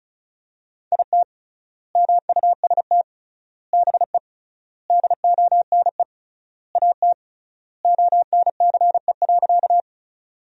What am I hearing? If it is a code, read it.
Morse code, heard as a keyed tone, about 36 wpm, 700 Hz. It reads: IT MUST BE DONE AT ONCE.
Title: 3x Word-Spacing - Sentences from Top 500 Words - 35wpm